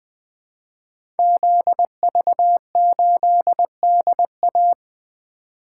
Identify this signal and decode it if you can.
Morse code, heard as a keyed tone, about 20 wpm, 700 Hz. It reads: ZV8DA
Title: Code Groups - 5 Letter:Numbers - 20wpm